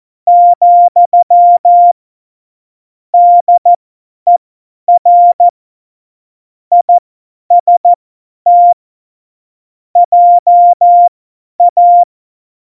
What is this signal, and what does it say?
Morse code, heard as a keyed tone, about 14 wpm, spaced as for 10 wpm, 700 Hz. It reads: , DER IST JA
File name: Morsekurs / Witze / Witz 04. September 2018, 50 F70